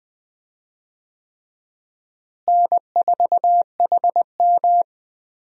Morse code, heard as a keyed tone, about 20 wpm, 700 Hz. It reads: N4HM
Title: USA Call Signs - 4 Characters - Repeated 3x - 20wpm